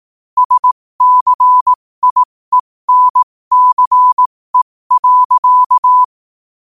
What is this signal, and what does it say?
Morse code, heard as a keyed tone, about 18 wpm, spaced as for 15 wpm, 1000 Hz. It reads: SCIENCE.
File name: 15wpm